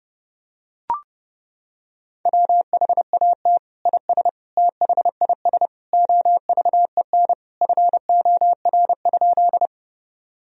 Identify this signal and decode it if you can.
Morse code, heard as a keyed tone, about 32 wpm, 700 Hz. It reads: WHAT IS THIS OVEN FOR?
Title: Sentences from Top 1000 Words - Review Part 4 - 30wpm